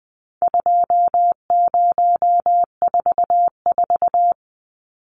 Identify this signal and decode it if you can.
Morse code, heard as a keyed tone, about 20 wpm, 700 Hz. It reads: 2044
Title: Numbers - 4-Digit 20wpm